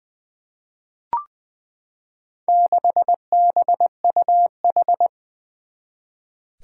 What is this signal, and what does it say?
Morse code, heard as a keyed tone, about 20 wpm, 700 Hz. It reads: 6BUH
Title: Four Challenging Letter-Number Combinations 20wpm